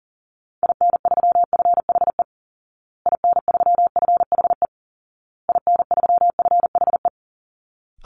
Morse code, heard as a keyed tone, about 40 wpm, 700 Hz. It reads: IN3FHE IN3FHE IN3FHE
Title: Call Signs Repeated 2x - 40wpm